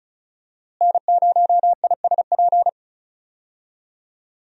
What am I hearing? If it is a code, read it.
Morse code, heard as a keyed tone, about 34 wpm, 700 Hz. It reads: N0ISP
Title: USA Call Signs - 5 Characters - Repeated 3x - 35wpm